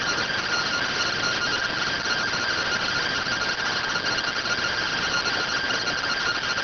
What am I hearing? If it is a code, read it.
Two-tone teleprinter signal, about 170 Hz shift.